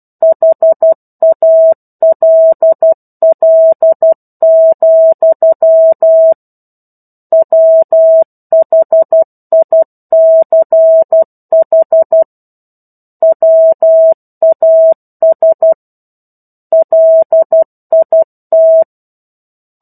Morse code, heard as a keyed tone, about 12 wpm, 650 Hz. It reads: HALL, WHICH WAS LIT